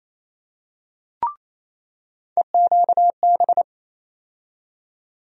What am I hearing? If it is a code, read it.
Morse code, heard as a keyed tone, about 28 wpm, 700 Hz. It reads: EQB
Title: USA Call Sign Postfixes - 3 Letters 28wpm